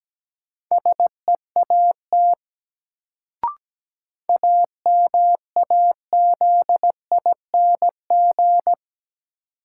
Morse code, heard as a keyed tone, about 17 wpm, 700 Hz. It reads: SEAT AMAZING
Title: Sentences from Top 1500 Words - New Word Drill - 17wpm